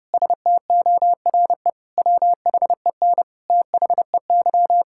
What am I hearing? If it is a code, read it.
Morse code, heard as a keyed tone, about 32 wpm, 700 Hz. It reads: STORE WHEN THEY
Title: Sentences from Top 1000 Words - Review Part 5 - 30wpm